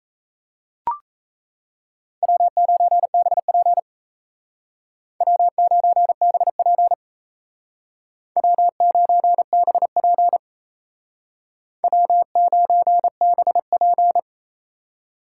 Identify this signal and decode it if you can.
Morse code, heard as a keyed tone, about 36 wpm, 700 Hz. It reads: W9BP W9BP W9BP W9BP